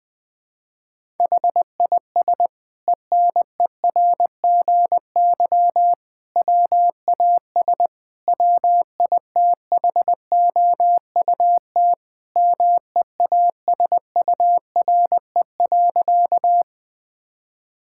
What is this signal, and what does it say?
Morse code, heard as a keyed tone, about 20 wpm, 700 Hz. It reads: HIS ENERGY WAS WITHOUT MEASURE.
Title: Sentences from Top 1000 Words - Review Part 6 - 20wpm